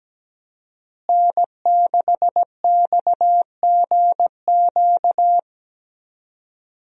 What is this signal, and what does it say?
Morse code, heard as a keyed tone, about 17 wpm, 700 Hz. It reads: N6XGQ
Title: USA Call Signs - 5 Characters - Repeated 3x - 17wpm